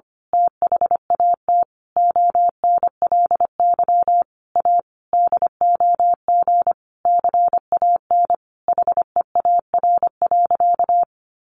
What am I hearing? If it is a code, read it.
Morse code, heard as a keyed tone, about 25 wpm, 700 Hz. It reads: THAT ONLY A DOG CAN HEAR.
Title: A Sound Of Thunder 25wpm